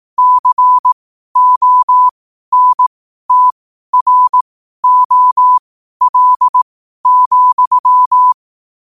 Morse code, heard as a keyed tone, about 18 wpm, spaced as for 12 wpm, 1000 Hz. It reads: CONTROL,
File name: qotdfiles / 12wpm